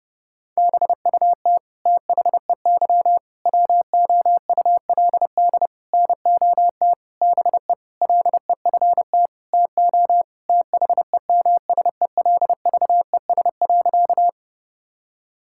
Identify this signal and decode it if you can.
Morse code, heard as a keyed tone, about 30 wpm, 700 Hz. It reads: BUT THEY WOULD NOT BE LEFT TO THEMSELVES.
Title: Sentences from Top 1000 Words - Review Part 5 - 30wpm